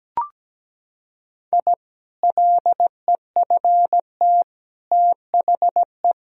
Morse code, heard as a keyed tone, about 17 wpm, 700 Hz. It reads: I LEFT THE
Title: Sentences from Top 1000 Words - Review Part 2 - 17wpm